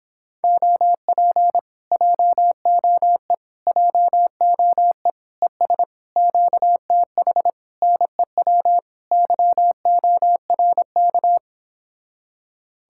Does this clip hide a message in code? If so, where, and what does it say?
Yes: Morse code from 0.4 to 11.4 s, heard as a keyed tone, about 26 wpm, 700 Hz. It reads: OP JOE JOE ES QTH NEW YORK